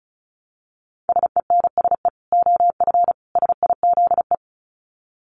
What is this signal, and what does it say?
Morse code, heard as a keyed tone, about 34 wpm, 700 Hz. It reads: SENSE OF SIZE
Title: Dit Challenge 35wpm